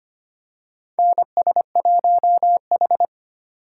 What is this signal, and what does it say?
Morse code, heard as a keyed tone, about 25 wpm, 700 Hz. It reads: NS1H
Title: USA Call Signs - 4 Characters - 3x Character Spacing - 25wpm